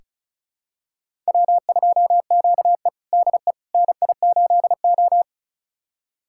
Morse code, heard as a keyed tone, about 34 wpm, 700 Hz. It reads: W2QE DE NI8O
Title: USA Callsign DE Callsign - 4 Characters - 35wpm